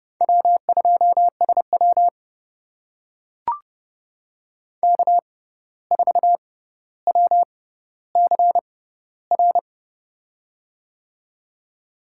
Morse code, heard as a keyed tone, about 32 wpm, spaced as for 15 wpm, 700 Hz. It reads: W2SW K4WCR
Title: USA Call Signs - 6x Character Spacing - 30wpm